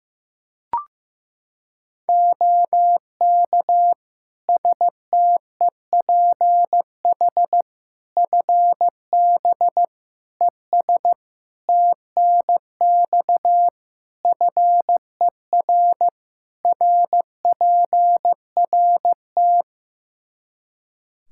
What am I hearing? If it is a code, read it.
Morse code, heard as a keyed tone, about 15 wpm, 700 Hz. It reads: OK STEPH FB ES TNX FER RPRT
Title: Practice QSO - Exchange 2 - Info - 15wpm